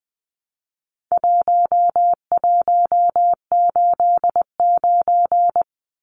Morse code, heard as a keyed tone, about 20 wpm, 700 Hz. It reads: 1189
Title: Numbers - 4-Digit 20wpm